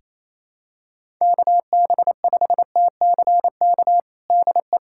Morse code, heard as a keyed tone, about 27 wpm, 700 Hz. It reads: KB5TCK DE
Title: USA Callsign DE Callsign - 6 Characters - 28wpm